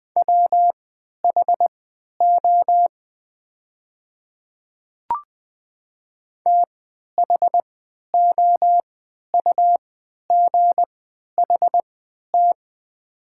Morse code, heard as a keyed tone, about 20 wpm, spaced as for 11 wpm, 690 Hz. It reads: WHO THOUGHT
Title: 3x Character Spacing - Top 500 Words - 20wpm